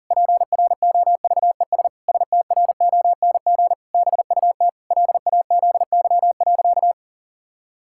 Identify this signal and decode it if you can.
Morse code, heard as a keyed tone, about 37 wpm, 700 Hz. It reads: PROVES STRONG BUT LAZY.
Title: Pangram Challenge v2 - 40wpm